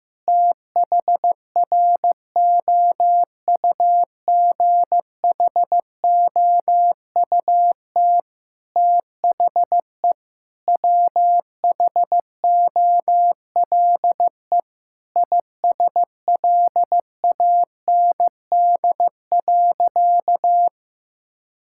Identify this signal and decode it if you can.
Morse code, heard as a keyed tone, about 15 wpm, 700 Hz. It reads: THROUGHOUT THE WHOLE ISLAND.
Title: Sentences from Top 1000 Words - Review Part 5 - 15wpm